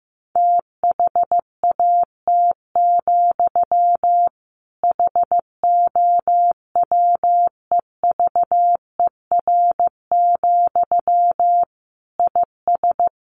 Morse code, heard as a keyed tone, about 14 wpm, 700 Hz. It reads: THAT, HOWEVER, IS
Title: Sentences from Top 1500 Words 15wpm